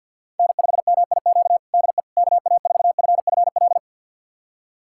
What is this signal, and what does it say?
Morse code, heard as a keyed tone, about 47 wpm, 700 Hz. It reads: N5MIQ DE KA4FFL
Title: USA Callsign DE Callsign 50wpm